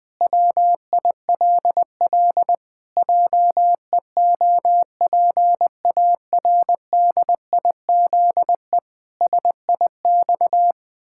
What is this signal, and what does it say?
Morse code, heard as a keyed tone, about 20 wpm, 700 Hz. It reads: WILL JEOPARDIZE SIX
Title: Pangram Challenge v2 - 20wpm